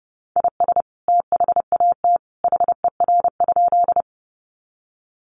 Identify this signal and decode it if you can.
Morse code, heard as a keyed tone, about 28 wpm, 700 Hz. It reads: IS THAT HER?
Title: Sentences from Top 100 Words 30wpm